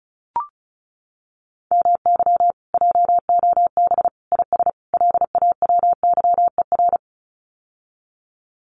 Morse code, heard as a keyed tone, about 34 wpm, 700 Hz. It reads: MY JOB IS LAWYER
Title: Common QSO Phrases Encore 35wpm